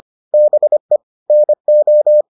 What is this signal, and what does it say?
Morse code, heard as a keyed tone, about 25 wpm, 600 Hz. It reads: BE NO